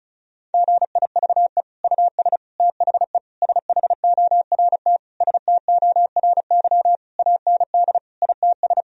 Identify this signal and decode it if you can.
Morse code, heard as a keyed tone, about 34 wpm, 700 Hz. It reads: GIVE US THE SHORT STORY AND ITS
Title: Sentences from Top 1000 Words - Review Part 4 - 35wpm